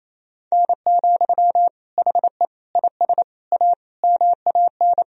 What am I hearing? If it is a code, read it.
Morse code, heard as a keyed tone, about 27 wpm, 700 Hz. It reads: N, HE IS A MAN
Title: Sentences from Top 400 Words and Abbreviations 28wpm